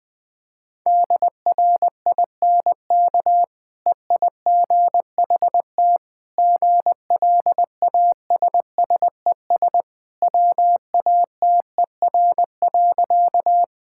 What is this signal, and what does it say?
Morse code, heard as a keyed tone, about 20 wpm, 700 Hz. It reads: DRINK EIGHT GLASSES WATER.
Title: Easy Sentences with Hints - 20wpm